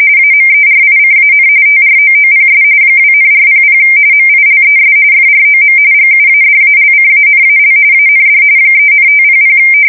RTTY, about 45.45 baud. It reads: AMERICAN TROOPS 2,400 MILES FROM GUADALCANAL AND 3,500
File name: fsk170 / USS-New-Jersey